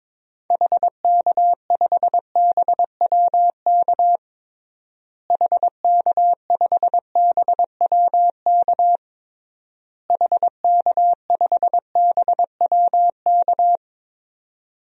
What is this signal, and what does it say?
Morse code, heard as a keyed tone, about 22 wpm, 700 Hz. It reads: HK5BWK HK5BWK HK5BWK
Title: DXCC-100 Call Signs - 6 Characters - Repeated 3x - 22wpm